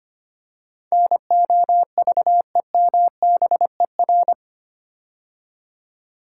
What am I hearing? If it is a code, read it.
Morse code, heard as a keyed tone, about 25 wpm, 690 Hz. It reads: NOVEMBER
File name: 8 Letter Words - First 3 Letters Given - 25wpm